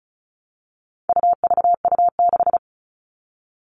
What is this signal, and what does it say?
Morse code, heard as a keyed tone, about 34 wpm, 700 Hz. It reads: UVU6
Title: All Cut Numbers - 4-Digit 35wpm